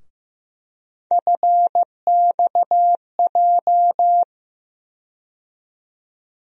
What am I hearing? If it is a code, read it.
Morse code, heard as a keyed tone, about 15 wpm, 700 Hz. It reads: FXJ
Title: USA Call Sign Postfixes - 3 Letters 15wpm